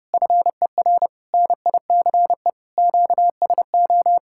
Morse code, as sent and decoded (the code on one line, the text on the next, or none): ..-. .-. / -. .. -.-. / --.- ... ---
FER NICE QSO